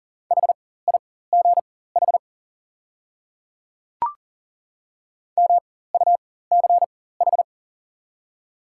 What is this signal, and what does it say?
Morse code, heard as a keyed tone, about 40 wpm, spaced as for 19 wpm, 700 Hz. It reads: HIGH MUCH
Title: 4x Character Spacing - Top 300 Words - 40wpm